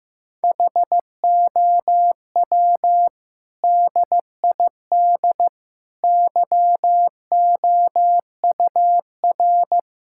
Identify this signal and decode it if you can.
Morse code, heard as a keyed tone, about 15 wpm, 700 Hz. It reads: HOW DID YOUR
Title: Sentences from Top 1000 Words - Review Part 6 - 15wpm